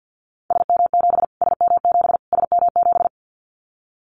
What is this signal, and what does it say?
Morse code, heard as a keyed tone, about 50 wpm, 700 Hz. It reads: SM7 SM7 SM7
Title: Callsign Prefixes for Top 25 DXCC - Repeated 3x - 50wpm